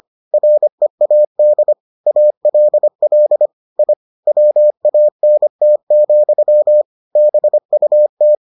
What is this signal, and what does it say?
Morse code, heard as a keyed tone, about 25 wpm, 600 Hz. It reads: READ ALL I WANT, BUT